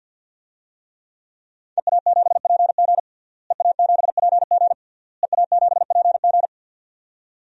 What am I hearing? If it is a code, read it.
Morse code, heard as a keyed tone, about 51 wpm, 700 Hz. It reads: EA7PG EA7PG EA7PG
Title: DXCC-100 Call Signs - 5 Characters - Repeated 3x - 50wpm